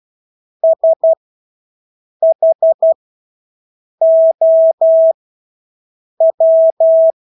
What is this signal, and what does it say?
Morse code, heard as a keyed tone, about 12 wpm, spaced as for 6 wpm, 650 Hz. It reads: SHOW
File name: Engelske ord 12-6 wpm